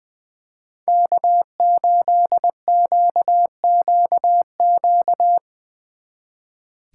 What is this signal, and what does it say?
Morse code, heard as a keyed tone, about 23 wpm, 700 Hz. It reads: K8QQQ
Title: Speed Racing - USA 1x3 Call Signs 20wpm